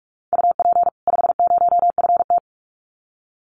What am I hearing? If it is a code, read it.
Morse code, heard as a keyed tone, about 44 wpm, 700 Hz. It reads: UP 50FT
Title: Common QSO Phrases 45wpm